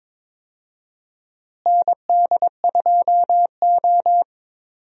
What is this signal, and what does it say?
Morse code, heard as a keyed tone, about 22 wpm, 700 Hz. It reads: ND2O
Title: USA Call Signs - 4 Characters - Repeated 2x - 22wpm